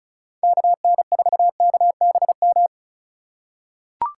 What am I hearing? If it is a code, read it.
Morse code, heard as a keyed tone, about 36 wpm, 700 Hz. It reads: KN4KBM
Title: USA Call Signs - 3x Character Spacing - 35wpm